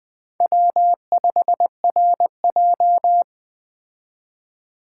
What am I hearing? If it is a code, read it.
Morse code, heard as a keyed tone, about 20 wpm, 700 Hz. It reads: W5RJ
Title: Speed Racing - USA 1x2 Call Signs 17wpm